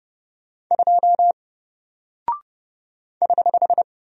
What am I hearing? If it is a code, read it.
Morse code, heard as a keyed tone, about 30 wpm, 700 Hz. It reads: 2 <HH>
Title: Single Character (Letter:Number:Punctuation:Prosign) Alt 30wpm